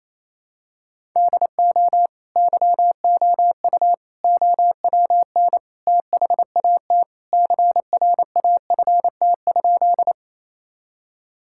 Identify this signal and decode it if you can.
Morse code, heard as a keyed tone, about 28 wpm, 700 Hz. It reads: DO YOU OWN THAT CRAFT?
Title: Sentences from Top 1000 Words - Review Part 6 - 28wpm